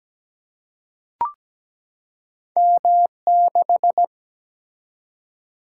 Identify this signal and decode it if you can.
Morse code, heard as a keyed tone, about 17 wpm, 700 Hz. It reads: M6
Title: Single-Letter Single-Number 17wpm